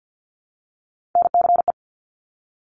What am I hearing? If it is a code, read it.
Morse code, heard as a keyed tone, about 50 wpm, 700 Hz. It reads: NCE
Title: Top Three Letter Combinations in Words 50wpm